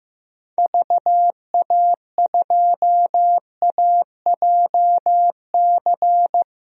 Morse code, heard as a keyed tone, about 16 wpm, 700 Hz. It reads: VA2AJC
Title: DXCC-100 Call Signs - 6 Characters - 15wpm